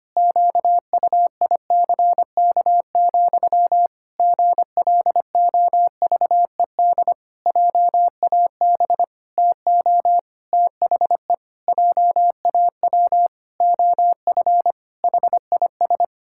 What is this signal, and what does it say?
Morse code, heard as a keyed tone, about 25 wpm, 700 Hz. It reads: QUICK, GLOVED JAB TO THE JAW OF HIS
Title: Pangram Challenge v2 - 25wpm